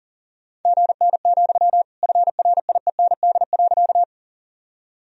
Morse code, heard as a keyed tone, about 40 wpm, 700 Hz. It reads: GN, FRIEND.
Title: Sentences from Top 300 Words and Abbreviations 40wpm